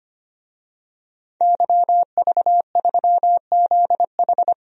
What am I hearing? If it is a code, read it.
Morse code, heard as a keyed tone, about 25 wpm, 700 Hz. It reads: YV3ZH
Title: DXCC-100 Call Signs - 5 Characters - 25wpm